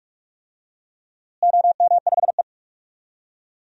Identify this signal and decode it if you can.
Morse code, heard as a keyed tone, about 46 wpm, 700 Hz. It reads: OM5E
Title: DXCC-100 Call Signs - 4 Characters - Repeated 2x - 45wpm